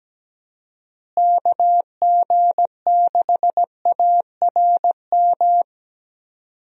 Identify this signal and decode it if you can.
Morse code, heard as a keyed tone, about 17 wpm, 700 Hz. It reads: KG6ARM
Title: USA Call Signs - 6 Characters - Repeated 2x - 17wpm